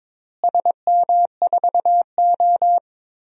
Morse code, heard as a keyed tone, about 22 wpm, 700 Hz. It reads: SM4O
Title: DXCC-100 Call Signs - 4 Characters - Repeated 3x - 22wpm